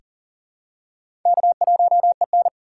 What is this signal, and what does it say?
Morse code, heard as a keyed tone, about 40 wpm, 700 Hz. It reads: K1EN
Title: USA Call Signs - 2x Character Spacing - 40wpm